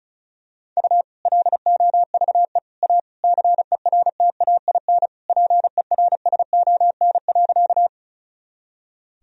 Morse code, heard as a keyed tone, about 34 wpm, 700 Hz. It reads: U LOVE A CERTAIN PERSON.
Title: Sentences from Top 500 Words and Abbreviations 35wpm